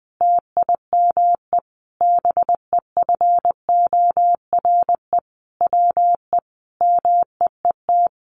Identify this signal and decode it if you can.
Morse code, heard as a keyed tone, about 20 wpm, 700 Hz. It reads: TIME BEFORE WE MEET
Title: Sentences from Top 1100 Words 20wpm